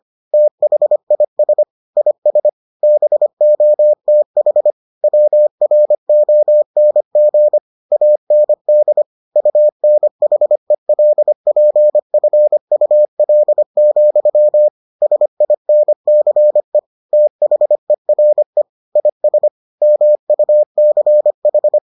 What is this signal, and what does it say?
Morse code, heard as a keyed tone, about 25 wpm, 600 Hz. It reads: THIS IS BOTH WRONG AND UNHELPFUL, SINCE THERE IS MUCH